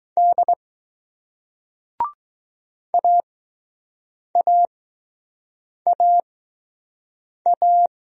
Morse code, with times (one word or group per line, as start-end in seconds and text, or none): 0.2-0.5 D
2.9-7.9 AAAA